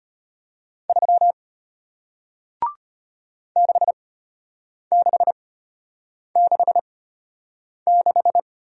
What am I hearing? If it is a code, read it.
Morse code, heard as a keyed tone, about 33 wpm, spaced as for 8 wpm, 700 Hz. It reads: 3 6666